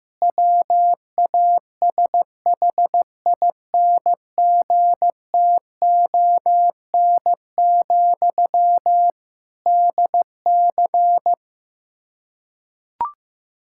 Morse code, heard as a keyed tone, about 15 wpm, 700 Hz. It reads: WASHINGTON, DC
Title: K1USN SST QSO Practice 15wpm